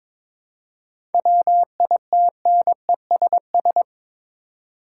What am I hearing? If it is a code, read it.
Morse code, heard as a keyed tone, about 22 wpm, 700 Hz. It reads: WITNESS
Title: Seven Letter Words - 22wpm